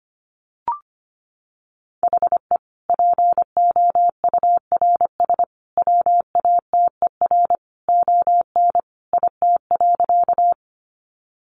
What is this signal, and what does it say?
Morse code, heard as a keyed tone, about 25 wpm, 700 Hz. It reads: HE POURS WATER ON IT.
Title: Additional Easy Phrases 25wpm